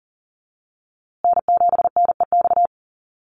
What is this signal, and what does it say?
Morse code, heard as a keyed tone, about 40 wpm, 700 Hz. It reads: N7NEX
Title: USA Call Signs Repeated 2x 40wpm